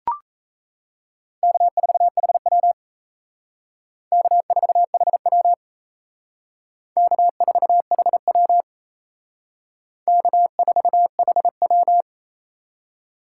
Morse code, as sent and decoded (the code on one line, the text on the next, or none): -.- ....- .... .-- / -.- ....- .... .-- / -.- ....- .... .-- / -.- ....- .... .--
K4HW K4HW K4HW K4HW